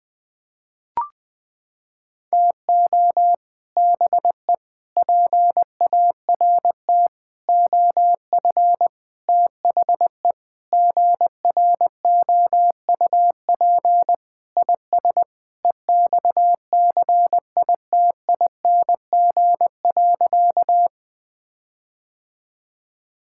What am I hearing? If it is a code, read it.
Morse code, heard as a keyed tone, about 20 wpm, 700 Hz. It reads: TO BE PART OF THE GROUP IS EXCITING.
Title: Sentences with 4-Word Phrases - 20wpm